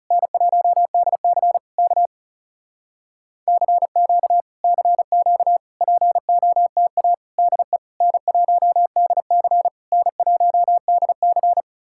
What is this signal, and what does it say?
Morse code, heard as a keyed tone, about 40 wpm, 700 Hz. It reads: N1DC K CQ CQ POTA DE N1DC N1DC